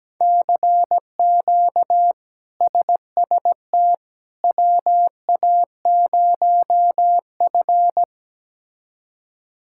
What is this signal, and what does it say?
Morse code, heard as a keyed tone, about 17 wpm, 700 Hz. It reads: CQ SST WA0F
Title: K1USN SST QSO Practice - 17wpm